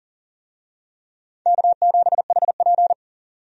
Morse code, heard as a keyed tone, about 40 wpm, 700 Hz. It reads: K7HP
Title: USA Call Signs - 4 Characters - 40wpm